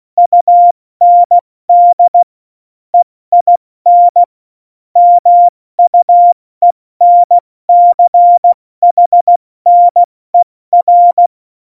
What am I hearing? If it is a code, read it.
Morse code, heard as a keyed tone, about 16 wpm, spaced as for 14 wpm, 700 Hz. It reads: UND EIN MUENCHNER
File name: Morsekurs / Witze / Witz 10.Dezember 2019, 70 F80